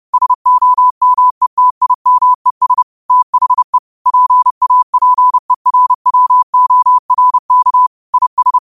Morse code, heard as a keyed tone, about 30 wpm, 1000 Hz. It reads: SOMETIMES THE PAPERWORK IS